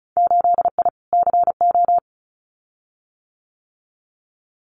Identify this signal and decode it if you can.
Morse code, heard as a keyed tone, about 36 wpm, 700 Hz. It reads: 8I CO
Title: Winter Field Day Exchanges Encore 35wpm